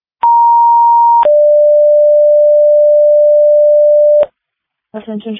Two-tone page, heard as a one-second tone, then a three-second tone, about 940 then 590 Hz. It is heard from 0.2 to 4.2 s.